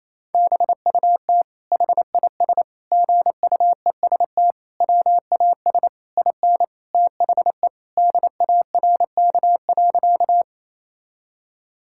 Morse code, heard as a keyed tone, about 28 wpm, 700 Hz. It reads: BUT HIS GUEST WAS IN THE DARK.
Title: Sentences from Top 1600 Words 28wpm